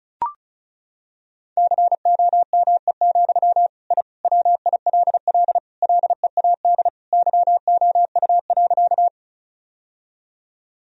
Morse code, heard as a keyed tone, about 36 wpm, 700 Hz. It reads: COME, I WILL LEAD YOU.
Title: Sentences from Top 1000 Words - Review Part 2 - 35wpm